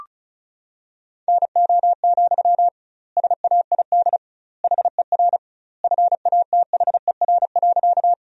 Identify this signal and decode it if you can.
Morse code, heard as a keyed tone, about 34 wpm, 700 Hz. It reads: NO, SAID HER FATHER.
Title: 2x Word-Spacing - Sentences from Top 300 Words - 35wpm